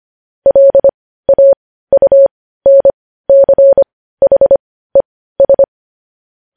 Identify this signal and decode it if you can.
Morse code, heard as a keyed tone, about 25 wpm, spaced as for 15 wpm, 550 Hz. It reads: LAUNCHES
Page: news headlines in morse code at 15 wpm